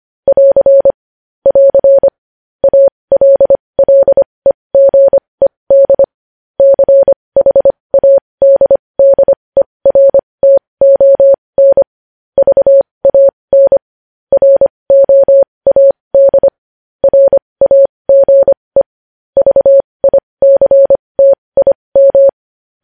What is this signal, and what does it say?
Morse code, heard as a keyed tone, about 25 wpm, spaced as for 20 wpm, 550 Hz. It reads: <AR> <AR> ALLEGED CHADDERTON VAN ROAD RAGE VICTIM
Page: News Headlines in Morse Code at 20 WPM